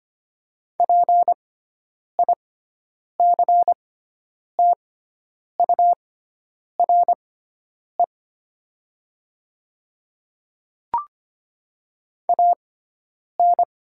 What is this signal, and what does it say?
Morse code, heard as a keyed tone, about 25 wpm, spaced as for 9 wpm, 700 Hz. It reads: PICTURE AN
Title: 6x Character Spacing - Top 300 Words - 25wpm